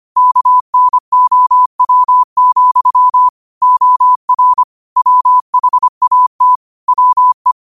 Morse code, heard as a keyed tone, about 25 wpm, 1000 Hz. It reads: KNOW, OR WHAT WE
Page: New quotes every day in morse code at 25 Words per minute.